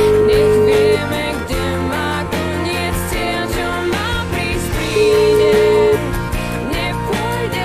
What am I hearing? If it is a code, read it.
Ringback tone (Europe, 425 Hz), with noise.